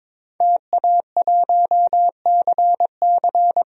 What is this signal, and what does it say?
Morse code, heard as a keyed tone, about 21 wpm, 700 Hz. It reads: TA1CC
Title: DXCC-100 Call Signs - 5 Characters - 22wpm